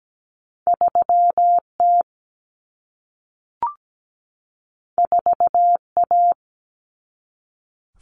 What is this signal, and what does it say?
Morse code, heard as a keyed tone, about 17 wpm, 700 Hz. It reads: 3T 4A
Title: Common Cut Numbers - 2-Digit 17wpm